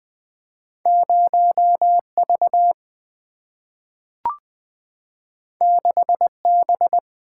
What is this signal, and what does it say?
Morse code, heard as a keyed tone, about 20 wpm, 690 Hz. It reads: 0V 6B
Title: Two Letter-Number Combinations - 20wpm